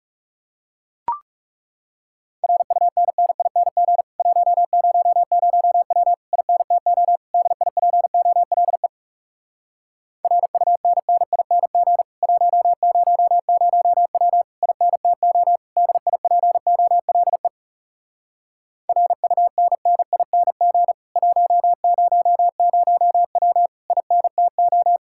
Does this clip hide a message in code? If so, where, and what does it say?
Yes: Morse code from 2.4 to 25.0 s, heard as a keyed tone, about 40 wpm, 700 Hz. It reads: RUNNING 100W INTO DIPOLE RUNNING 100W INTO DIPOLE RUNNING 100W INTO